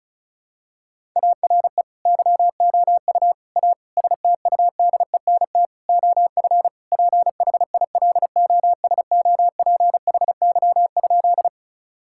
Morse code, heard as a keyed tone, about 36 wpm, 700 Hz. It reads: ARE YOU A STUDENT OF PHILOSOPHY?
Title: Sentences from Top 1000 Words - Review Part 6 - 35wpm